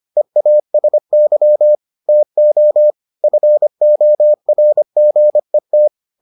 Morse code, heard as a keyed tone, about 25 wpm, 600 Hz. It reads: EASY TO FORGET